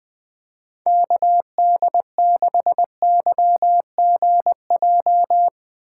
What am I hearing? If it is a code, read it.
Morse code, heard as a keyed tone, about 21 wpm, 700 Hz. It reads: KD6YGJ
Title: USA Call Signs - 3x Character Spacing - 20wpm